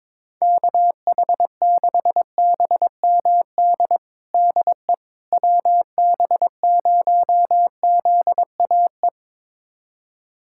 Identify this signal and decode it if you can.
Morse code, heard as a keyed tone, about 22 wpm, 700 Hz. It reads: KH6BMD DE WB0ZAE
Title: USA Callsign DE Callsign - 6 Characters - 22wpm